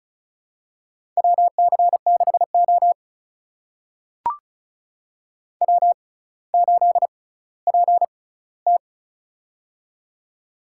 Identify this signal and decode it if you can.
Morse code, heard as a keyed tone, about 35 wpm, spaced as for 18 wpm, 700 Hz. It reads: WC6O W8PT
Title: USA Call Signs - 4 Characters - 6x Character Spacing - 35wpm